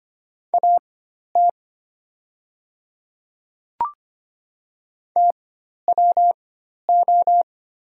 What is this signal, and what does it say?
Morse code, heard as a keyed tone, about 25 wpm, spaced as for 12 wpm, 690 Hz. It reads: AT TWO
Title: 4x Character Spacing - Top 100 Words - 25wpm